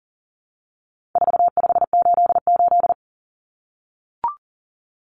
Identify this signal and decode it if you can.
Morse code, heard as a keyed tone, about 40 wpm, 700 Hz. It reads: 4588
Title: Common Cut Numbers - 4-Digit 40wpm